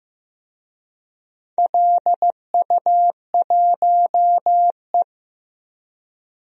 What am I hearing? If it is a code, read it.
Morse code, heard as a keyed tone, about 15 wpm, 700 Hz. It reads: LU1E
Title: DXCC-100 Call Signs - 4 Characters - Repeated 3x - 15wpm